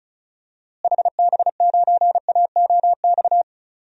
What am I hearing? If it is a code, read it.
Morse code, heard as a keyed tone, about 36 wpm, 700 Hz. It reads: HB9AOX
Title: DXCC-100 Call Signs - 6 Characters - Repeated 2x - 35wpm